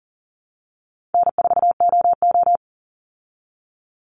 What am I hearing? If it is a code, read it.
Morse code, heard as a keyed tone, about 40 wpm, 700 Hz. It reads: N4OO
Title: Call Signs Repeated 3x - 40wpm